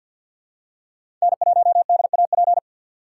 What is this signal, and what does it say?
Morse code, heard as a keyed tone, about 50 wpm, 700 Hz. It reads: N1DAP
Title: USA Call Signs - 3x Character Spacing - 50wpm